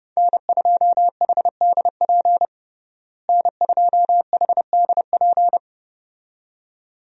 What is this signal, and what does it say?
Morse code, heard as a keyed tone, about 32 wpm, 700 Hz. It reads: N2HDP N2HDP
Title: USA Call Signs - 5 Characters - Repeated 2x - 30wpm